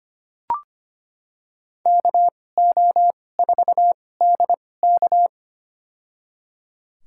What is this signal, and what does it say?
Morse code, heard as a keyed tone, about 25 wpm, spaced as for 18 wpm, 700 Hz. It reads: KO4DK
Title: USA Call Signs - 5 Characters - 2x Character Spacing - 25wpm